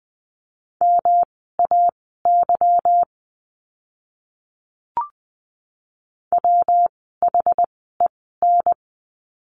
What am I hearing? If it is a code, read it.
Morse code, heard as a keyed tone, about 20 wpm, spaced as for 14 wpm, 700 Hz. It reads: MAY WHEN
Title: 146 Lesson 24f - Letter y - Words 20wpm